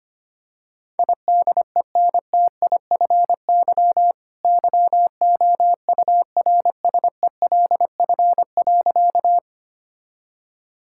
Morse code, heard as a keyed tone, about 25 wpm, 700 Hz. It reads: IDENTIFY YOURSELF.
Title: Sentences from Top 1000 Words - Review Part 6 - 25wpm